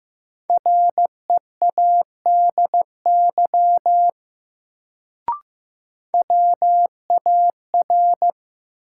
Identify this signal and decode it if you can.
Morse code, heard as a keyed tone, about 15 wpm, 700 Hz. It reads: READY WAR
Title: Sentences from Top 400 Words - New Word Drill - 15wpm